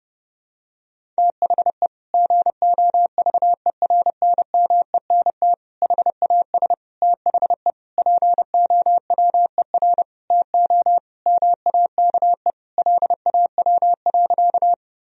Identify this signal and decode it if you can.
Morse code, heard as a keyed tone, about 32 wpm, 700 Hz. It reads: THE GOVERNMENT HAS THE POWER TO MAKE LAW.
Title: Sentences from Top 1000 Words - Review Part 3 - 30wpm